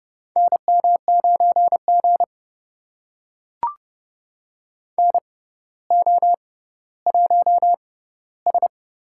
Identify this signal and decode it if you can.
Morse code, heard as a keyed tone, about 30 wpm, spaced as for 15 wpm, 700 Hz. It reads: NM9G NO1S
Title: USA Call Signs - 4 Characters - 6x Character Spacing - 30wpm